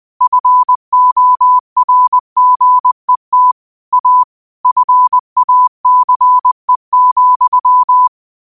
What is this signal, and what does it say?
Morse code, heard as a keyed tone, about 20 wpm, 1000 Hz. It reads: FORGET A FACE,
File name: qotdfiles / 20wpm